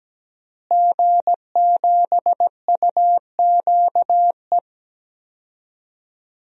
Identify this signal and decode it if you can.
Morse code, heard as a keyed tone, about 17 wpm, 700 Hz. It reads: G7UQE
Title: DXCC-100 Call Signs - 5 Characters - Repeated 2x - 17wpm